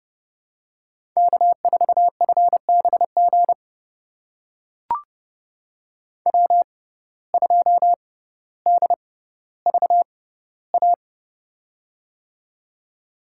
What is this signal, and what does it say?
Morse code, heard as a keyed tone, about 30 wpm, spaced as for 15 wpm, 700 Hz. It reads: K4FBG W2DVA
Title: USA Call Signs - 5 Characters - 6x Character Spacing - 30wpm